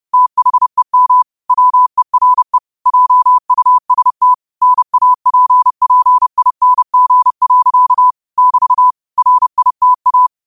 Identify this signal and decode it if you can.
Morse code, heard as a keyed tone, about 30 wpm, 1000 Hz. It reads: THEM WERE JUST NAPPING. <BT> RITA
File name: qotdfiles / 30wpm